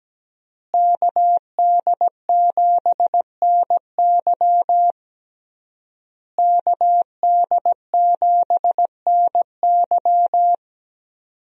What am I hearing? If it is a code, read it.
Morse code, heard as a keyed tone, about 17 wpm, 700 Hz. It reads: KD7NY KD7NY
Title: USA Call Signs - 5 Characters - Repeated 3x - 17wpm